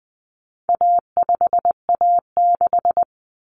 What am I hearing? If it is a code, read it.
Morse code, heard as a keyed tone, about 20 wpm, 700 Hz. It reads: A5A6
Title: Common Cut Numbers - 4-Digit 20wpm